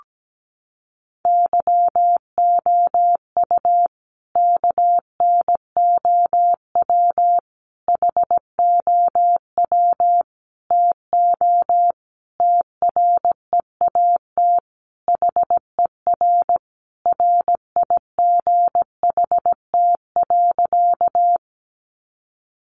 Morse code, heard as a keyed tone, about 17 wpm, 700 Hz. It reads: YOU KNOW HOW TO TREAT HER RIGHT.
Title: Sentences from Top 1300 Words 17wpm